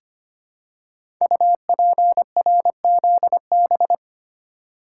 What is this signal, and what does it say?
Morse code, heard as a keyed tone, about 25 wpm, 700 Hz. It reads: UPRZB
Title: Code Groups - 5 Letters - 25wpm